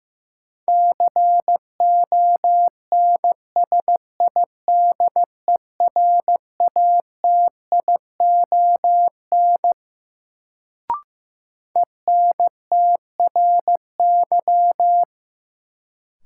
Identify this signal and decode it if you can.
Morse code, heard as a keyed tone, about 15 wpm, 700 Hz. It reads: CONSIDERATION ENTRY
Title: Sentences from Top 1500 Words - New Word Drill - 15wpm